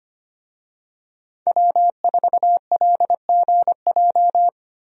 Morse code, heard as a keyed tone, about 25 wpm, 700 Hz. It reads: W4LGJ
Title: USA Call Signs - 5 Characters - Repeated 2x - 25wpm